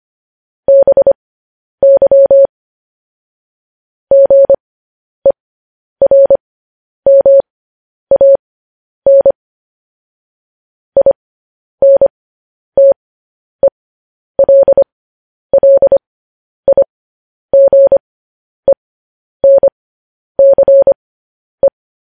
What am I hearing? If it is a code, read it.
Morse code, heard as a keyed tone, about 25 wpm, spaced as for 10 wpm, 550 Hz. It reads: BY GERMAN INTELLIGENCE
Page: news-headlines-in-morse-code-at-10-wpm